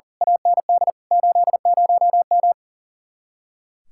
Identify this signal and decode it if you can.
Morse code, heard as a keyed tone, about 39 wpm, 700 Hz. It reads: AND 80M
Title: Speed-Racing Common QSO Phrases - 40wpm